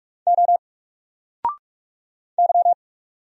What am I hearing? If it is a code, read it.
Morse code, heard as a keyed tone, about 46 wpm, 700 Hz. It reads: O Y
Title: Single Letter-Number 45wpm